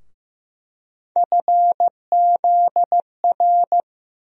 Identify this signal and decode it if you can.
Morse code, heard as a keyed tone, about 15 wpm, 700 Hz. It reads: FZR
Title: USA Call Sign Postfixes - 3 Letters 15wpm